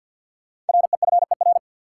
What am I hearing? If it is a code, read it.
Morse code, heard as a keyed tone, about 53 wpm, 700 Hz. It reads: REFER
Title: Sentences from Top 1000 Words - New Word Drill - 50wpm